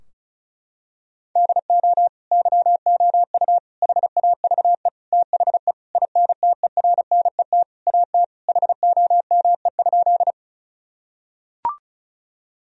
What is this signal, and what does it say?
Morse code, heard as a keyed tone, about 36 wpm, 700 Hz. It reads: DO YOU HAVE THE INTERNET AT HOME?
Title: Sentences from Top 1000 Words - Review Part 4 - 35wpm